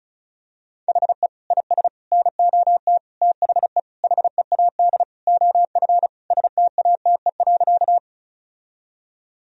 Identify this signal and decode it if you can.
Morse code, heard as a keyed tone, about 36 wpm, 700 Hz. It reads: HE IS NOT THE HEAD OF STATE.
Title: Sentences from Top 1000 Words - Review Part 2 - 35wpm